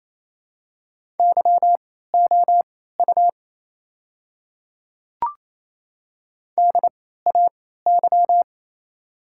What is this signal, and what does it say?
Morse code, heard as a keyed tone, about 28 wpm, spaced as for 16 wpm, 700 Hz. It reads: YOU DAY
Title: 3x Character Spacing - Top 100 Words - 28wpm